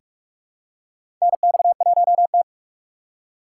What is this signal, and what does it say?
Morse code, heard as a keyed tone, about 46 wpm, 700 Hz. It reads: NX1T
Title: USA Call Signs - 4 Characters - 2x Character Spacing - 45wpm